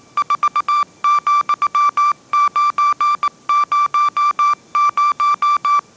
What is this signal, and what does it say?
Morse code, heard as a keyed tone, about 19 wpm, 1170 Hz. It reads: 4,900
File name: 20wpm